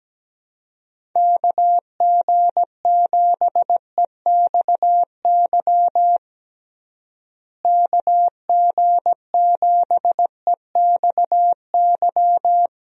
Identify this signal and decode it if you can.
Morse code, heard as a keyed tone, about 17 wpm, 700 Hz. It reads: KG7EXY KG7EXY
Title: USA Call Signs - 6 Characters - Repeated 2x - 17wpm